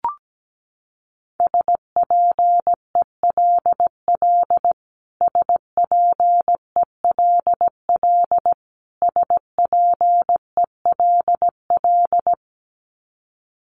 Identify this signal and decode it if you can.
Morse code, heard as a keyed tone, about 17 wpm, 700 Hz. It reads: SPELL SPELL SPELL
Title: Top 500 Words Repeated 17wpm